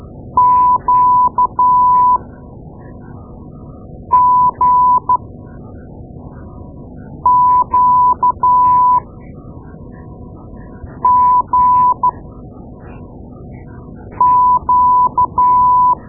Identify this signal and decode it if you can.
Morse code, heard as a keyed tone, about 13 wpm, spaced as for 4 wpm, 1000 Hz. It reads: QGQGQ
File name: Morse Lessons / Morse Lessons / Tape-2A